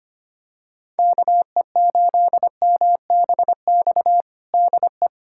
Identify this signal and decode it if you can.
Morse code, heard as a keyed tone, about 25 wpm, 700 Hz. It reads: KE8MBX DE
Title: USA Callsign DE Callsign 25wpm